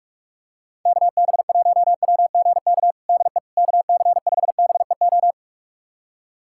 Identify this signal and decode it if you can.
Morse code, heard as a keyed tone, about 44 wpm, 700 Hz. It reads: KB1WGK DE KC5BEO
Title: USA Callsign DE Callsign - 6 Characters - 45wpm